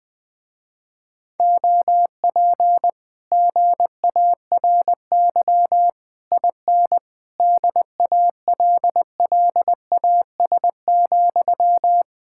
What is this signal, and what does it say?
Morse code, heard as a keyed tone, about 20 wpm, 700 Hz. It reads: OP GARY IN DALLAS,
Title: Speed-Racing Common QSO Phrases - 20wpm